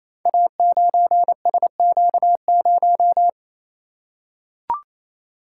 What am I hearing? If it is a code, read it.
Morse code, heard as a keyed tone, about 28 wpm, 700 Hz. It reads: A9SQ0
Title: Code Groups - 5 Letter:Numbers - 28wpm